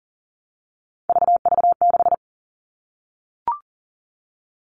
Morse code, heard as a keyed tone, about 40 wpm, 700 Hz. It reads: VV6
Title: All Cut Numbers - 3-Digit 40wpm